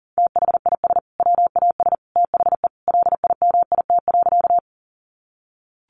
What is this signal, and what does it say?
Morse code, heard as a keyed tone, about 40 wpm, 700 Hz. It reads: THIS WAS THE LIMIT.
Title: Sentences from Top 1500 Words 40wpm